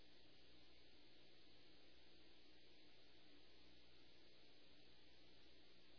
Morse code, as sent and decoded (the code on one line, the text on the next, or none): none